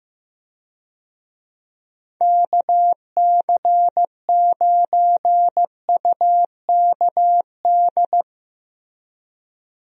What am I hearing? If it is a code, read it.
Morse code, heard as a keyed tone, about 15 wpm, 700 Hz. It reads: KC9UKD
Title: USA Call Signs - 6 Characters - 15wpm